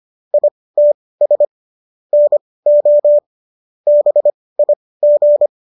Morse code, heard as a keyed tone, about 25 wpm, spaced as for 18 wpm, 590 Hz. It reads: ITS NO BIG